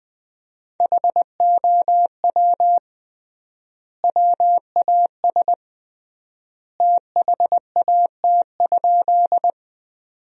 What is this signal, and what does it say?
Morse code, heard as a keyed tone, about 21 wpm, 700 Hz. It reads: HOW WAS THAT?
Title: 3x Word-Spacing - Sentences from Top 100 Words - 20wpm